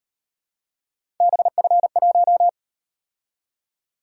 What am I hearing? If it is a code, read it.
Morse code, heard as a keyed tone, about 38 wpm, 700 Hz. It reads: BF1